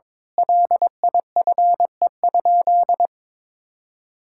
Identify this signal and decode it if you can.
Morse code, heard as a keyed tone, about 22 wpm, 700 Hz. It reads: LIFE?
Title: Sentences from Top 1000 Words - Review Part 6 - 22wpm